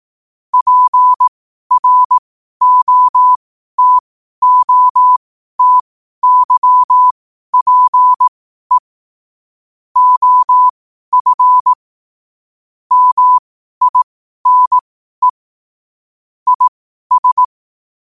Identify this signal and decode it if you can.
Morse code, heard as a keyed tone, about 18 wpm, spaced as for 12 wpm, 1000 Hz. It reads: PROTOTYPE OF MINE IS
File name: qotdfiles / 12wpm